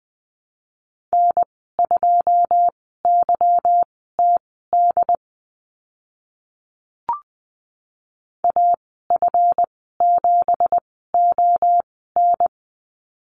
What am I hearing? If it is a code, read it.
Morse code, heard as a keyed tone, about 20 wpm, spaced as for 14 wpm, 700 Hz. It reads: N2YTD AF7ON
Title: 193 Lesson 31g - Letter v - Call Signs 20wpm